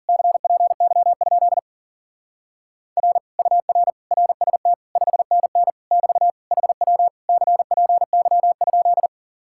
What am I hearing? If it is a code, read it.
Morse code, heard as a keyed tone, about 47 wpm, 700 Hz. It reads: CPY? R UR RST 5NN <BT> HW CPY?
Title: Speed-Racing Common QSO Phrases - 40wpm